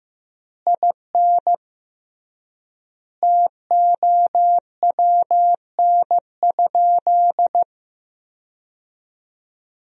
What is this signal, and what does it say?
Morse code, heard as a keyed tone, about 15 wpm, 700 Hz. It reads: IN TOWN?
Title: 3x Word-Spacing - Sentences from Top 500 Words - 15wpm